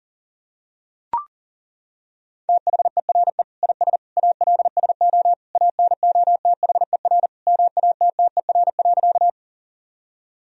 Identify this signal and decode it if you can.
Morse code, heard as a keyed tone, about 40 wpm, 700 Hz. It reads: THERE IS ALSO ANOTHER MATTER.
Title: Sentences from Top 1000 Words - Review Part 1 - 40wpm